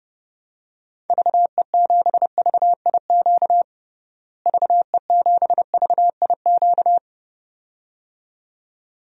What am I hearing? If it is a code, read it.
Morse code, heard as a keyed tone, about 30 wpm, 700 Hz. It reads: VE7VIQ VE7VIQ
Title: DXCC-100 Call Signs - 6 Characters - Repeated 2x - 30wpm